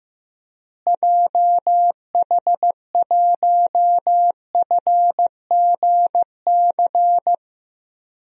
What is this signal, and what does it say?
Morse code, heard as a keyed tone, about 15 wpm, 700 Hz. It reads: JH1FGC
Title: DXCC-100 Call Signs - 6 Characters - Repeated 3x - 15wpm